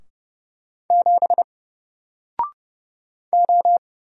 Morse code, heard as a keyed tone, about 30 wpm, 700 Hz. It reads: 7 O
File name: Single-Letter-Number-30wpm